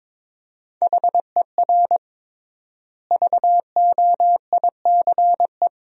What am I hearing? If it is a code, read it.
Morse code, heard as a keyed tone, about 22 wpm, 700 Hz. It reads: HER VOICE
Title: 3x Word-Spacing - Sentences from Top 500 Words - 22wpm